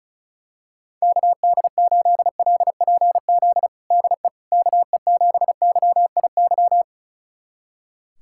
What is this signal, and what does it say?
Morse code, heard as a keyed tone, about 36 wpm, 700 Hz. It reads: KD8LPZ DE KE7YIY
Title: USA Callsign DE Callsign - 6 Characters - 35wpm